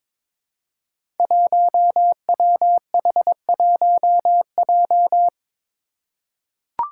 Morse code, heard as a keyed tone, about 22 wpm, 700 Hz. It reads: 1WH1J
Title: Five Challenging Letter-Number Combinations 22wpm